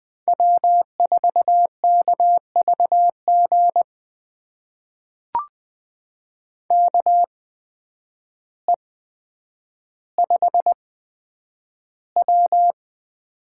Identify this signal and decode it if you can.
Morse code, heard as a keyed tone, about 20 wpm, spaced as for 9 wpm, 700 Hz. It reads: W4KVG KE5W
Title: USA Call Signs - 5 Characters - 8x Character Spacing - 20wpm